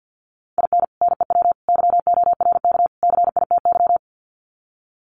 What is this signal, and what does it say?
Morse code, heard as a keyed tone, about 50 wpm, 700 Hz. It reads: IN NEW YORK CITY
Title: Sets of 4 Words 50wpm